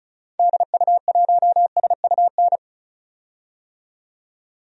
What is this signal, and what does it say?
Morse code, heard as a keyed tone, about 36 wpm, 700 Hz. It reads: DU1SUN
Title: DXCC-100 Call Signs - 6 Characters - 35wpm